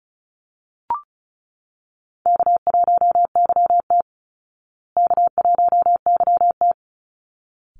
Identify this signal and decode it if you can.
Morse code, heard as a keyed tone, about 34 wpm, 700 Hz. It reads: K1YT K1YT
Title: Easy Call Signs Repeated 2x - 35wpm